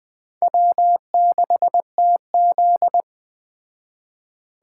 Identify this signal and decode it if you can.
Morse code, heard as a keyed tone, about 20 wpm, 700 Hz. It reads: W6TZ
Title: Speed Racing - USA 1x2 Call Signs 17wpm